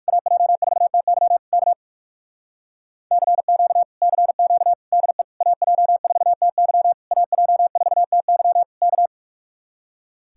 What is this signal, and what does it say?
Morse code, heard as a keyed tone, about 62 wpm, spaced as for 54 wpm, 700 Hz. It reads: AJ4TY K CQ CQ DE AJ4TY AJ4TY K
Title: Speed-Racing Common QSO Phrases Encore - 45wpm